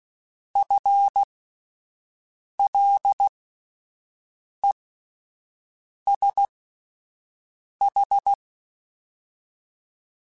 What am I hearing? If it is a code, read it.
Morse code, heard as a keyed tone, about 16 wpm, spaced as for 5 wpm, 800 Hz. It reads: FLESH
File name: Spaced / FL Runs sp mp3s / FL Runs sp @10